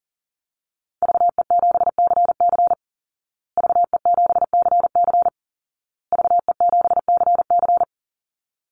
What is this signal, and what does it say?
Morse code, heard as a keyed tone, about 37 wpm, 700 Hz. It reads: VE7CC VE7CC VE7CC
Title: Call Signs Repeated 2x - 40wpm